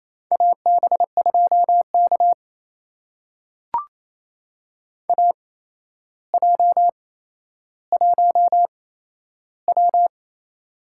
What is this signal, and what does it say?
Morse code, heard as a keyed tone, about 28 wpm, spaced as for 12 wpm, 700 Hz. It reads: AB2K AJ1W
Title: USA Call Signs - 4 Characters - 8x Character Spacing - 28wpm